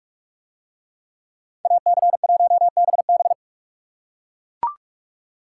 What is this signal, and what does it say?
Morse code, heard as a keyed tone, about 39 wpm, 700 Hz. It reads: AC1BB